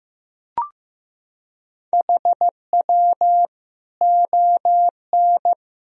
Morse code, heard as a keyed tone, about 15 wpm, 700 Hz. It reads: HW ON
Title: Sentences from Top 200 Words and Abbreviations 15wpm